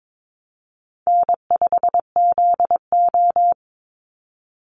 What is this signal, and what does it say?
Morse code, heard as a keyed tone, about 22 wpm, 700 Hz. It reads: N5ZO
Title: Call Signs - 22wpm